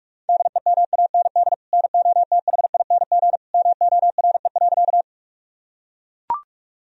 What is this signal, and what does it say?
Morse code, heard as a keyed tone, about 42 wpm, 700 Hz. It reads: DEMAND NOTHING MORE.
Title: Sentences from Top 1000 Words - Review Part 4 - 45wpm